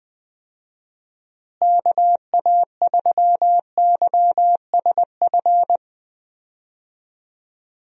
Morse code, heard as a keyed tone, about 21 wpm, 700 Hz. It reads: KA3YSF
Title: USA Call Signs - 6 Characters - Repeated 3x - 20wpm